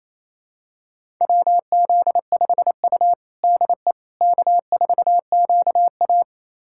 Morse code, heard as a keyed tone, about 28 wpm, 700 Hz. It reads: WZ5U DE K4QA
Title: USA Callsign DE Callsign - 4 Characters - 28wpm